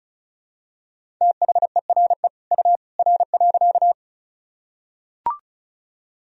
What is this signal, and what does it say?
Morse code, heard as a keyed tone, about 36 wpm, 700 Hz. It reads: THERE U R.
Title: Sentences from Top 100 Words and Abbreviations 35wpm